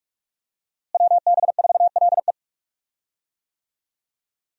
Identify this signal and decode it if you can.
Morse code, heard as a keyed tone, about 46 wpm, 700 Hz. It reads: WB4LE
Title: USA Call Signs - 5 Characters - Repeated 2x - 45wpm